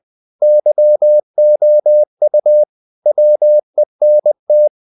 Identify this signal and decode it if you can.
Morse code, heard as a keyed tone, about 20 wpm, 600 Hz. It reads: YOU WENT
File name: the old man and the sea / the old man and the sea part 1 20wpm